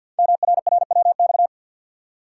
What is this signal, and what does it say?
Morse code, heard as a keyed tone, about 50 wpm, 700 Hz. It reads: MRRWX
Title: Code Groups - 5 Letters - 50wpm